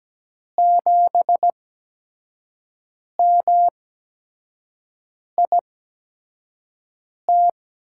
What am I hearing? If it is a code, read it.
Morse code, heard as a keyed tone, about 17 wpm, spaced as for 5 wpm, 700 Hz. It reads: 7MIT
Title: USA Call Signs - 5 Characters - 8x Character Spacing - 17wpm